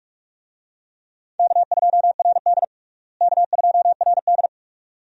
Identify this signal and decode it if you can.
Morse code, heard as a keyed tone, about 46 wpm, 700 Hz. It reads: K2RD K2RD
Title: USA Call Signs - 4 Characters - Repeated 2x - 45wpm